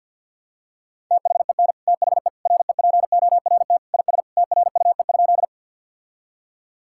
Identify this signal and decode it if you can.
Morse code, heard as a keyed tone, about 52 wpm, 700 Hz. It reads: THEN THE REPORT IS TRUE?
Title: Sentences from Top 1000 Words - Review Part 6 - 50wpm